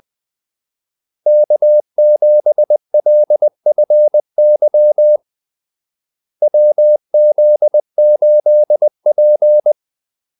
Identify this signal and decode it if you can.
Morse code, heard as a keyed tone, about 20 wpm, 600 Hz. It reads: K7LFY WZ8P
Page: Kaldesignaler 20 wpm